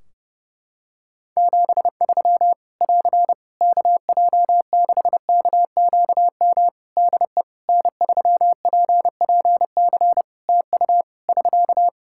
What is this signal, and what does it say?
Morse code, heard as a keyed tone, about 30 wpm, 700 Hz. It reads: 73 <AR> KJ6KQM DE N3PPC TU <SK>
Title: K1USN SST QSO Practice 30wpm